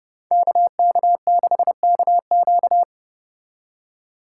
Morse code, heard as a keyed tone, about 33 wpm, 700 Hz. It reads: KK6KQ